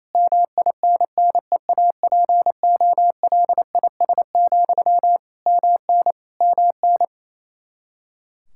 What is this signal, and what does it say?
Morse code, heard as a keyed tone, about 28 wpm, 700 Hz. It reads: MINNEAPOLIS, MN MN
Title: Practice QSO - Exchange 1 - Beginning - 28wpm